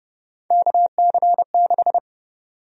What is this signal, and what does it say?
Morse code, heard as a keyed tone, about 30 wpm, 700 Hz. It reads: KC6
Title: USA Call Sign Prefixes - 30wpm